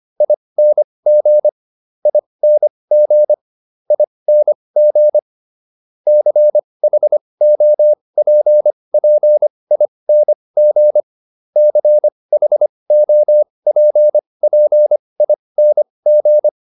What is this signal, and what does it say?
Morse code, heard as a keyed tone, about 25 wpm, spaced as for 20 wpm, 600 Hz. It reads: ING ING ING CHOPPING CHOPPING